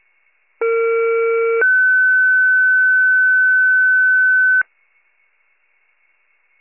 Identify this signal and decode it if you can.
Two-tone page, heard as a one-second tone, then a three-second tone, about 460 then 1500 Hz. It is heard from 0.6 to 4.6 s.
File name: pager / 21 FIRE / wcalert